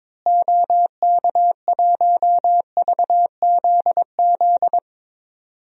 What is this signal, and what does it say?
Morse code, heard as a keyed tone, about 22 wpm, 700 Hz. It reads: OK1VZZ
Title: DXCC-100 Call Signs - 6 Characters - Repeated 2x - 22wpm